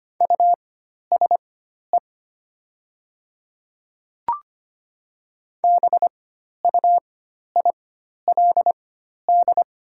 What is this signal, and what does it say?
Morse code, heard as a keyed tone, about 25 wpm, spaced as for 12 wpm, 700 Hz. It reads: USE BUILD
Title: 4x Character Spacing - Top 200 Words - 25wpm